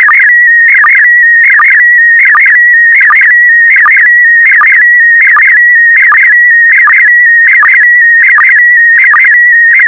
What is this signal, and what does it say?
SSTV (slow-scan television), PD 180.
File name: SSTV PD180 Leo